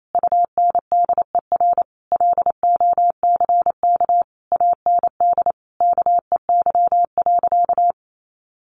Morse code, heard as a keyed tone, about 28 wpm, 700 Hz. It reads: UNDER LOCK AND KEY.
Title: Sentences from Top 1400 Words 28wpm